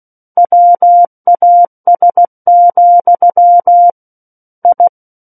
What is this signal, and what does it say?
Morse code, heard as a keyed tone, about 16 wpm, 700 Hz. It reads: WAS, I
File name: Book One / 07-Chapter 7 - 16 WPM